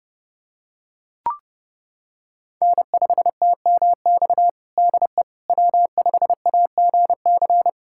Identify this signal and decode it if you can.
Morse code, heard as a keyed tone, about 32 wpm, 700 Hz. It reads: N5TMX DE W5AGC
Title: USA Callsign DE Callsign - 5 Characters - 30wpm